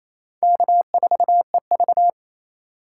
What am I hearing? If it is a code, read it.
Morse code, heard as a keyed tone, about 28 wpm, 700 Hz. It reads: K4EV
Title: USA Call Signs - 4 Characters - 2x Character Spacing - 28wpm